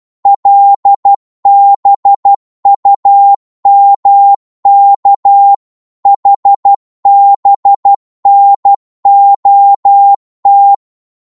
Morse code, heard as a keyed tone, about 12 wpm, 800 Hz. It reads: LBUMK HBNOT
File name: Morse / ltr-12wpm